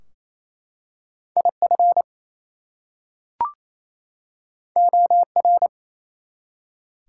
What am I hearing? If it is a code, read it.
Morse code, heard as a keyed tone, about 27 wpm, 700 Hz. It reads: IF OR
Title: Two Letter Words - 28wpm